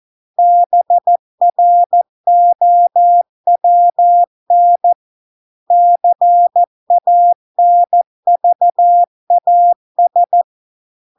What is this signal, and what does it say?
Morse code, heard as a keyed tone, about 14 wpm, 700 Hz. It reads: BROWN CANVAS